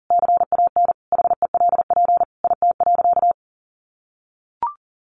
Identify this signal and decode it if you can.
Morse code, heard as a keyed tone, about 40 wpm, 700 Hz. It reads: CAN HELP IT.
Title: Sentences from Top 1100 Words 40wpm